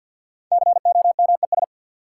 Morse code, heard as a keyed tone, about 53 wpm, 700 Hz. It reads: COMES
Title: Sets of 3 Words - New Word Drill - 50wpm